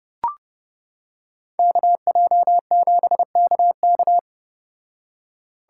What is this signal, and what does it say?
Morse code, heard as a keyed tone, about 29 wpm, 700 Hz. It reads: KJ7KK